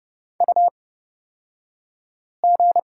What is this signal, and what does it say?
Morse code, heard as a keyed tone, about 30 wpm, spaced as for 5 wpm, 700 Hz. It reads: UG
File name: Lesson 38d - Letter q - Two Characters 30wpm